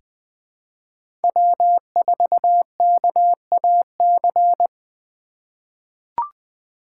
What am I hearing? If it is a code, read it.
Morse code, heard as a keyed tone, about 20 wpm, 700 Hz. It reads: W4KAC
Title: 2023 SOTA Chaser Call Signs - 20wpm